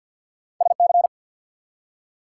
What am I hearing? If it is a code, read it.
Morse code, heard as a keyed tone, about 52 wpm, 700 Hz. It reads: S C
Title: US States and Canadian Provinces 50wpm